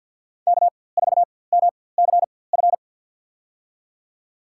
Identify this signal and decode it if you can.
Morse code, heard as a keyed tone, about 53 wpm, spaced as for 23 wpm, 700 Hz. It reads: K4MCF
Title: USA Call Signs - 5 Characters - 4x Character Spacing - 50wpm